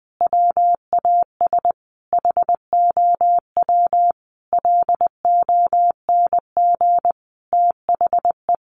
Morse code, heard as a keyed tone, about 20 wpm, 700 Hz. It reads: WAS HOW LONG THE
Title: Sentences from Top 1400 Words 20wpm